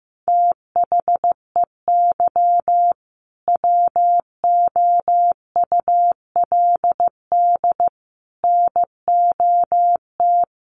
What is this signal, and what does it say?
Morse code, heard as a keyed tone, about 15 wpm, 700 Hz. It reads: THEY WOULD NOT
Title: Sentences from Top 1400 Words 15wpm